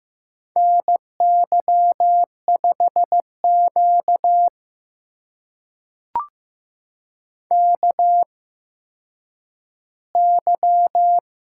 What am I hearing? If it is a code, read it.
Morse code, heard as a keyed tone, about 15 wpm, spaced as for 9 wpm, 700 Hz. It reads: NY5Q KY
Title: USA Call Signs - 4 Characters - 8x Character Spacing - 15wpm